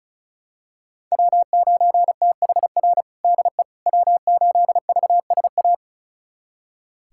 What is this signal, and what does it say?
Morse code, heard as a keyed tone, about 36 wpm, 700 Hz. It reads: W9THR DE W8VSA
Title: USA Callsign DE Callsign - 5 Characters - 35wpm